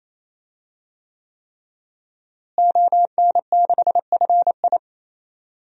Morse code, heard as a keyed tone, about 27 wpm, 700 Hz. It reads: ON6FI
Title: DXCC-100 Call Signs - 5 Characters - Repeated 3x - 28wpm